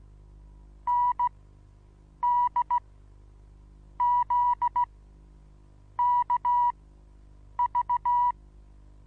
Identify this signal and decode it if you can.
Morse code, heard as a keyed tone, about 16 wpm, spaced as for 7 wpm, 970 Hz. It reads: NDZKV
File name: Onlinelehrgang / morsen / DJ4UF Morsen25